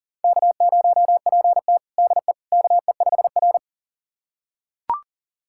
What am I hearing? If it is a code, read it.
Morse code, heard as a keyed tone, about 40 wpm, 690 Hz. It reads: K0PT DE KE5R
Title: USA Callsign DE Callsign - 4 Characters - 40wpm